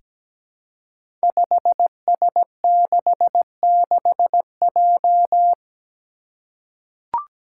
Morse code, heard as a keyed tone, about 17 wpm, 700 Hz. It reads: 5S66J
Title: Five Challenging Letter-Number Combinations 17wpm